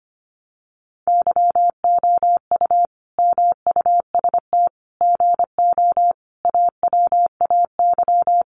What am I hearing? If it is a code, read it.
Morse code, heard as a keyed tone, about 25 wpm, 700 Hz. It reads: YOU MUST GO AWAY
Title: Sentences from Top 1100 Words 25wpm